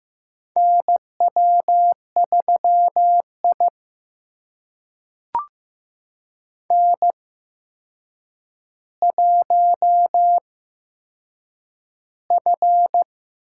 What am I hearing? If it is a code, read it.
Morse code, heard as a keyed tone, about 15 wpm, spaced as for 7 wpm, 700 Hz. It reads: NW3I N1F
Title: USA Call Signs - 4 Characters - 8x Character Spacing - 15wpm